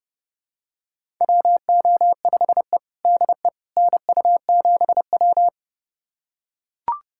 Morse code, heard as a keyed tone, about 30 wpm, 700 Hz. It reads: WO5E DE NU7W
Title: USA Callsign DE Callsign - 4 Characters - 30wpm